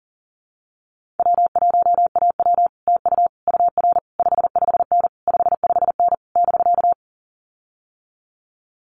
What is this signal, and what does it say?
Morse code, heard as a keyed tone, about 40 wpm, 700 Hz. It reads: W1AW TU UR 55N 55N <BK>
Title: Common QSO Phrases Encore 40wpm